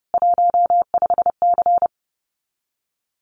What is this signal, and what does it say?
Morse code, heard as a keyed tone, about 30 wpm, 700 Hz. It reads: 15C
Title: Common QSO Phrases 30wpm